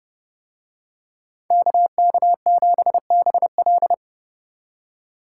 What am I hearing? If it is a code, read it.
Morse code, heard as a keyed tone, about 30 wpm, 700 Hz. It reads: KK7BL